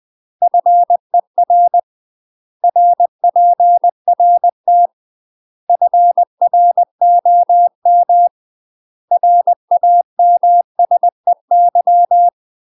Morse code, heard as a keyed tone, about 20 wpm, 700 Hz. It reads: FER RPRT FROM RAMSEY
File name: UK MASTER QSOs TRAINING/20WPM